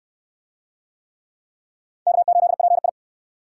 Morse code, heard as a keyed tone, about 67 wpm, 700 Hz. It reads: K8PI